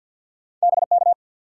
Morse code, heard as a keyed tone, about 53 wpm, 700 Hz. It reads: BK
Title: Top 90 QSO Elements 50wpm